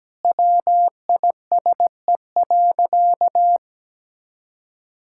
Morse code, heard as a keyed tone, about 17 wpm, 700 Hz. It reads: WISE.
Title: Sentences from Top 1600 Words 17wpm